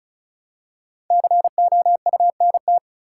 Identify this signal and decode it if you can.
Morse code, heard as a keyed tone, about 34 wpm, 700 Hz. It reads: COUNT
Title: Five Letter Words - 35wpm